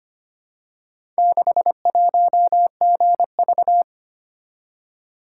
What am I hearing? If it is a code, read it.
Morse code, heard as a keyed tone, about 25 wpm, 700 Hz. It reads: 61GV
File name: practice / 25wpm / Four Letter-Number Combinations - 25wpm